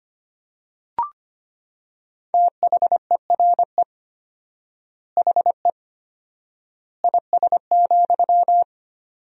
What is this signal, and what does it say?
Morse code, heard as a keyed tone, about 25 wpm, 700 Hz. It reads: THERE HE IS,
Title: 4x Word-Spacing - Sentences from Top 100 Words - 25wpm